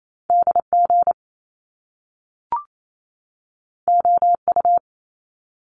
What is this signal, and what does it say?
Morse code, heard as a keyed tone, about 28 wpm, 700 Hz. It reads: DG OU
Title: Top Two Letter Combinations in Words 28wpm